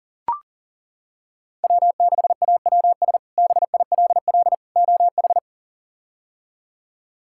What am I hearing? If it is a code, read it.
Morse code, heard as a keyed tone, about 40 wpm, 700 Hz. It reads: W6AWS BILL OH
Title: K1USN SST QSO Practice - 40wpm